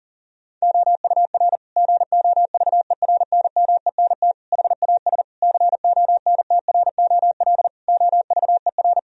Morse code, heard as a keyed tone, about 40 wpm, 700 Hz. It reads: OUR GOVERNMENT HAS CONTROL OVER
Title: Sentences from Top 1000 Words - Review Part 4 - 40wpm